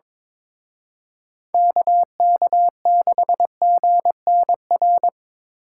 Morse code, heard as a keyed tone, about 25 wpm, 700 Hz. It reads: KK6GNR
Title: Speed Racing - USA 2x3 Call Signs 22wpm